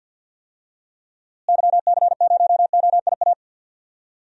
Morse code, heard as a keyed tone, about 48 wpm, 700 Hz. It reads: YC0OIA
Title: DXCC-100 Call Signs - 6 Characters - Repeated 2x - 50wpm